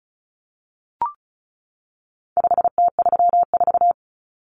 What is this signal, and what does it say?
Morse code, heard as a keyed tone, about 34 wpm, 700 Hz. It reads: 5T34
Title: Common Cut Numbers - 4-Digit 35wpm